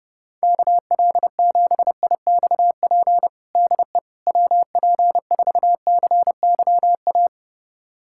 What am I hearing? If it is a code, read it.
Morse code, heard as a keyed tone, about 32 wpm, 700 Hz. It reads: KL7IXP DE WP4CYA
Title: USA Callsign DE Callsign - 6 Characters - 30wpm